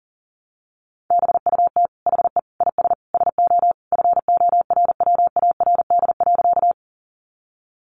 Morse code, heard as a keyed tone, about 40 wpm, 700 Hz. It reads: BUT HE IS SO FORWARD.
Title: Sentences from Top 1300 Words 40wpm